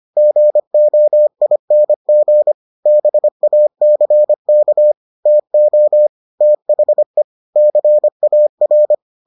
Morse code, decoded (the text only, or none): GOING BACK TO THE CAR